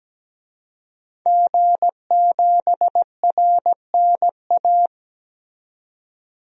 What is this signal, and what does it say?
Morse code, heard as a keyed tone, about 17 wpm, 700 Hz. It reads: G7RNA
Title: DXCC-100 Call Signs - 5 Characters - 17wpm